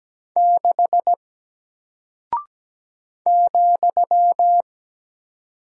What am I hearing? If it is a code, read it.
Morse code, heard as a keyed tone, about 17 wpm, 700 Hz. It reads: 6 ,
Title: Single Character (Letter:Number:Punctuation:Prosign) Alt 17wpm